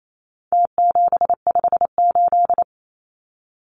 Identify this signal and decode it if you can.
Morse code, heard as a keyed tone, about 28 wpm, 700 Hz. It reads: T758
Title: Numbers - Contest Style - 4-Digit 28wpm